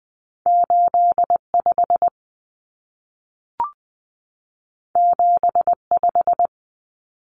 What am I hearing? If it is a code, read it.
Morse code, heard as a keyed tone, about 20 wpm, 700 Hz. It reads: 85 75
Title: Common Cut Numbers - 2-Digit 20wpm